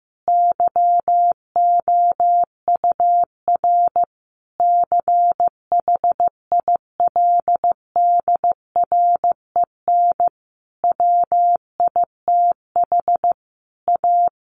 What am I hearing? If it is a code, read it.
Morse code, heard as a keyed tone, about 15 wpm, 700 Hz. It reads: YOUR CHILDREN WITH A
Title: Sentences from Top 1400 Words 15wpm